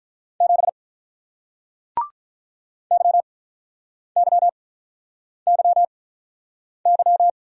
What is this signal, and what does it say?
Morse code, heard as a keyed tone, about 47 wpm, spaced as for 9 wpm, 700 Hz. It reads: 7 YYYY